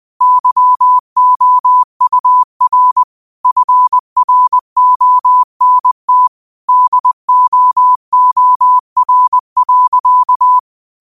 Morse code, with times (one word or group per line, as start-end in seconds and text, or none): 0.2-3.0 YOUR
3.5-6.3 FRONT
6.7-10.6 DOOR.